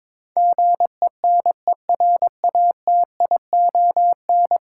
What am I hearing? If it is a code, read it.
Morse code, heard as a keyed tone, about 22 wpm, 700 Hz. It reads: GENERATION
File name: practice / 22wpm / 10 Letter Words - First 3 Letters Given - 22wpm